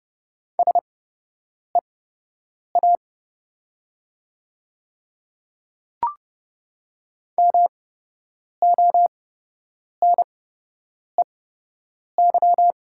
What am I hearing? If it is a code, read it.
Morse code, heard as a keyed tone, about 30 wpm, spaced as for 8 wpm, 700 Hz. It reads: SEA MONEY
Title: 8x Character Spacing - Top 500 Words - 30wpm